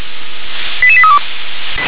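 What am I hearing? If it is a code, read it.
Five-tone selective call (ZVEI), heard as quick runs of five tones, tones of about 70 ms. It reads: 88832
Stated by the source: ZVEI five-tone call 88771 received on SDR.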